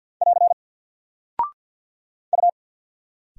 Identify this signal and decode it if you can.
Morse code, heard as a keyed tone, about 48 wpm, 700 Hz. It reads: <AR> U
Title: Single Character (Letter:Number:Punctuation:Prosign) 50wpm